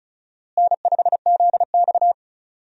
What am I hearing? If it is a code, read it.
Morse code, heard as a keyed tone, about 36 wpm, 700 Hz. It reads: N5ZX
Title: USA Call Signs - 4 Characters - Repeated 3x - 35wpm